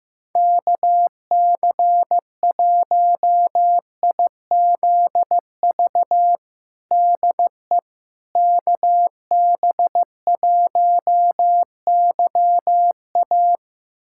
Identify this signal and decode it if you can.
Morse code, heard as a keyed tone, about 15 wpm, 700 Hz. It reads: KC1IZV DE KB1YA
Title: USA Callsign DE Callsign 15wpm